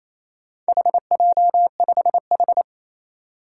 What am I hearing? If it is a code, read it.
Morse code, heard as a keyed tone, about 28 wpm, 700 Hz. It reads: HJ5H
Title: Four Challenging Letter-Number Combinations 28wpm